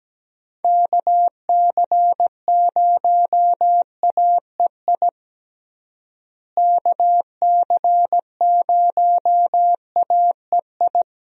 Morse code, heard as a keyed tone, about 17 wpm, 700 Hz. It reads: KC0AEI KC0AEI
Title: USA Call Signs - 6 Characters - Repeated 2x - 17wpm